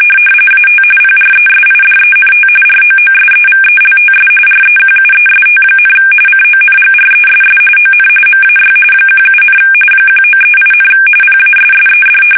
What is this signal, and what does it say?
RTTY, about 45.45 baud. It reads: AT PRESIDENT ROOSEVELTS REQUEST, CONGRESS ON MAY 17, 1938, PASSED
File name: fsk850 / battleshipnj